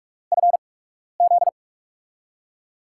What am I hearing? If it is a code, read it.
Morse code, heard as a keyed tone, about 44 wpm, spaced as for 12 wpm, 700 Hz. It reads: FZ
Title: USA Call Signs - 6 Characters - 8x Character Spacing - 45wpm